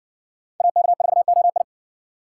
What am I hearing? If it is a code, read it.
Morse code, heard as a keyed tone, about 60 wpm, 700 Hz. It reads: AK4OI